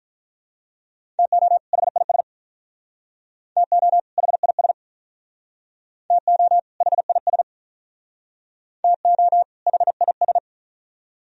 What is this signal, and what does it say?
Morse code, heard as a keyed tone, about 45 wpm, 700 Hz. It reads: TO HIS TO HIS TO HIS TO HIS